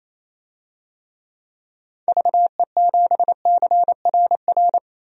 Morse code, heard as a keyed tone, about 28 wpm, 700 Hz. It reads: VE7CRR
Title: DXCC-100 Call Signs - 6 Characters - 28wpm